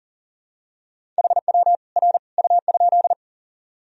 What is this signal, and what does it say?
Morse code, heard as a keyed tone, about 39 wpm, 700 Hz. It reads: HW R U?
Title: Sentences from Top 100 Words and Abbreviations 40wpm